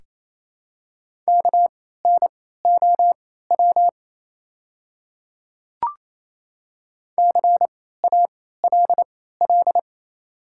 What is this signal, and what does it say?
Morse code, heard as a keyed tone, about 28 wpm, spaced as for 16 wpm, 700 Hz. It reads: KNOW CALL
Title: 3x Character Spacing - Top 100 Words - 28wpm